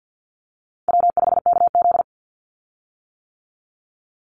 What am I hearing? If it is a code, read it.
Morse code, heard as a keyed tone, about 50 wpm, 700 Hz. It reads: W5KZ
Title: USA Call Signs 50wpm